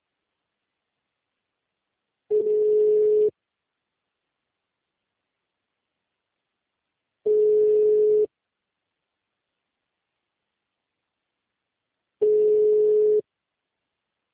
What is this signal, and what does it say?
Ringback tone (Europe, 425 Hz).